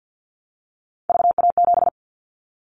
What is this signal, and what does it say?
Morse code, heard as a keyed tone, about 50 wpm, 700 Hz. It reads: VA7
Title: Callsign Prefixes for Top 50 DXCC - 50wpm